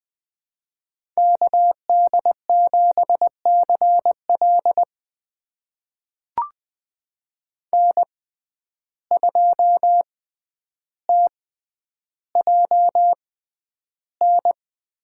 Morse code, heard as a keyed tone, about 20 wpm, spaced as for 10 wpm, 700 Hz. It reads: KD7CL N2TJN
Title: USA Call Signs - 5 Characters - 6x Character Spacing - 20wpm